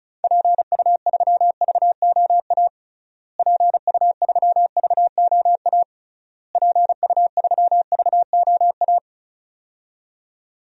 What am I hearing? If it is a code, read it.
Morse code, heard as a keyed tone, about 36 wpm, 700 Hz. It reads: PU3VOA PU3VOA PU3VOA
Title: DXCC-100 Call Signs - 6 Characters - Repeated 3x - 35wpm